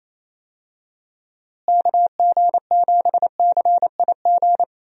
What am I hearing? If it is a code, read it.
Morse code, heard as a keyed tone, about 28 wpm, 700 Hz. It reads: KG7CIG
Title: USA Call Signs - 6 Characters - Repeated 2x - 28wpm